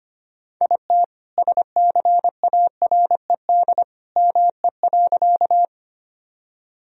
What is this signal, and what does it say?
Morse code, heard as a keyed tone, about 25 wpm, 700 Hz. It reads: IT SCARED ME.
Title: Sentences from Top 1000 Words - Review Part 4 - 25wpm